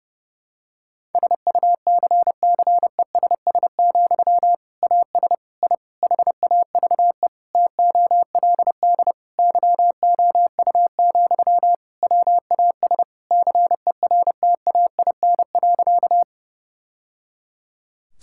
Morse code, heard as a keyed tone, about 30 wpm, 700 Hz. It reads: SUCCESS, AS I HAVE TOLD YOU, WAS CERTAIN.
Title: Sentences from Top 1000 Words - Review Part 6 - 30wpm